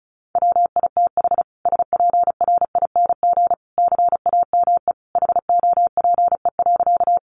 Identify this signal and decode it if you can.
Morse code, heard as a keyed tone, about 34 wpm, 700 Hz. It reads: WITH SPRING CAME HOPE.
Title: Sentences from Top 1200 Words 35wpm